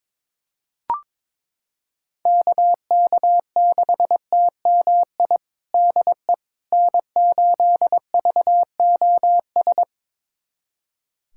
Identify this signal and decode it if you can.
Morse code, heard as a keyed tone, about 22 wpm, 700 Hz. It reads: KK6TMI DE N8VOS
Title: USA Callsign DE Callsign 22wpm